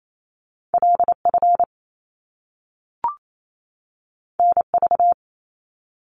Morse code, heard as a keyed tone, about 28 wpm, 700 Hz. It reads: LF NV
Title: Top Two Letter Combinations in Words 28wpm